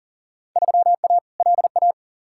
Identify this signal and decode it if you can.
Morse code, heard as a keyed tone, about 40 wpm, 700 Hz. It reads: 3A LA
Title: Field Day Exchanges - 40wpm